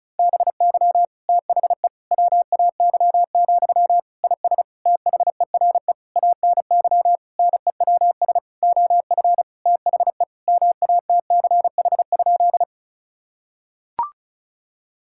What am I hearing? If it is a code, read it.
Morse code, heard as a keyed tone, about 36 wpm, 700 Hz. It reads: BY THE WAY, IS THERE ANY NEWS OF THE MATCH?
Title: Sentences from Top 1600 Words 35wpm